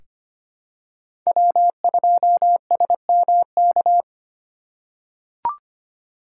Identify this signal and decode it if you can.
Morse code, heard as a keyed tone, about 25 wpm, 700 Hz. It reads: W2SMK
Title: USA Call Signs - 4x Character Spacing - 25wpm